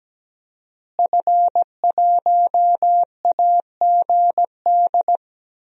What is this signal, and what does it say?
Morse code, heard as a keyed tone, about 17 wpm, 700 Hz. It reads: F1AGD
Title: DXCC-100 Call Signs - 5 Characters - Repeated 3x - 17wpm